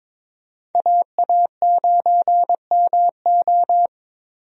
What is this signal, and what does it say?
Morse code, heard as a keyed tone, about 22 wpm, 700 Hz. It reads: AA9MO
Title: USA Call Signs - 5 Characters - Repeated 2x - 22wpm